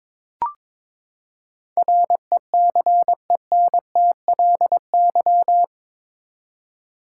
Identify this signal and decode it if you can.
Morse code, heard as a keyed tone, about 22 wpm, 700 Hz. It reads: RECENTLY
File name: practice / 22wpm / Eight Letter Words - 22wpm